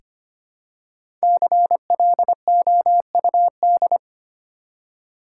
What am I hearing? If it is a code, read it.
Morse code, heard as a keyed tone, about 25 wpm, 700 Hz. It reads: CLOUD
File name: practice / 25wpm / Easy Words with Hints - 25wpm